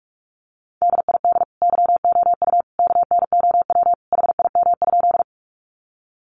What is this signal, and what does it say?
Morse code, heard as a keyed tone, about 44 wpm, 700 Hz. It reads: DID YOU KNOW HIM?
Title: Sentences from Top 100 Words 45wpm